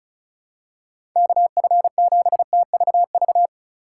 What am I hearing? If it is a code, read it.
Morse code, heard as a keyed tone, about 36 wpm, 690 Hz. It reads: KF7TVV
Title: USA Call Signs - 6 Characters - 2x Character Spacing - 35wpm